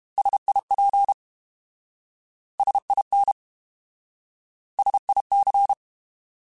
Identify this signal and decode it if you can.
Morse code, heard as a keyed tone, about 32 wpm, 800 Hz. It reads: SIP SIN SIC
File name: Alphabetical / Alphabetical mp3s / Ss @30